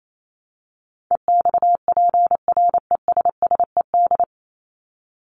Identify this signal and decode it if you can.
Morse code, heard as a keyed tone, about 28 wpm, 700 Hz. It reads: EXPRESSED
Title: ED Suffix - Top 100 Words with Base Words 28wpm